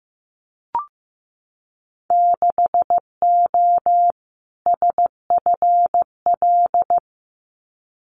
Morse code, heard as a keyed tone, about 15 wpm, 700 Hz. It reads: 6O SFL
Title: Winter Field Day Exchanges Encore 15wpm